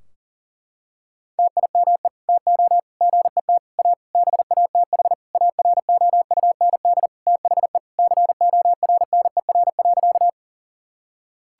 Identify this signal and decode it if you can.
Morse code, heard as a keyed tone, about 40 wpm, 700 Hz. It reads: TIME TO GET A BATH AROUND THE CORNER.
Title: Sentences from Top 1600 Words 40wpm